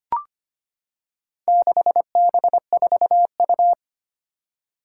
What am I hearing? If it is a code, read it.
Morse code, heard as a keyed tone, about 25 wpm, 700 Hz. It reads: 6B4U
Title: Four Challenging Letter-Number Combinations 25wpm